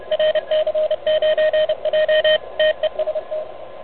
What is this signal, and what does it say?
Morse code, heard as a keyed tone, about 31 wpm, 590 Hz, with radio noise. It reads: RC9J TEST
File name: truSDX